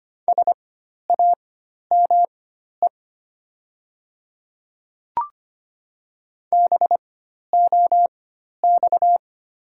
Morse code, heard as a keyed tone, about 25 wpm, spaced as for 12 wpm, 700 Hz. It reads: SAME BOX
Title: 4x Character Spacing - Top 500 Words - 25wpm